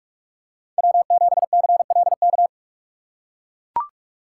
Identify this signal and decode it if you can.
Morse code, heard as a keyed tone, about 42 wpm, 700 Hz. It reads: W7CLK